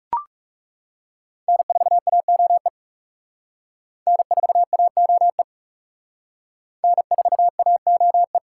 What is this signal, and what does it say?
Morse code, heard as a keyed tone, about 40 wpm, 700 Hz. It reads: N4AOE N4AOE N4AOE